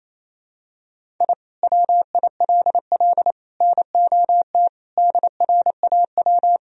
Morse code, heard as a keyed tone, about 28 wpm, 700 Hz. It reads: I WILL NOT DRAW
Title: Sentences from Top 1000 Words - Review Part 1 - 28wpm